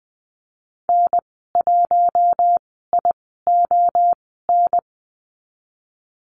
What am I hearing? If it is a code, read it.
Morse code, heard as a keyed tone, about 20 wpm, spaced as for 15 wpm, 700 Hz. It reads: N1ION
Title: 042 Lesson 8g - Number 1 - Call Signs 20wpm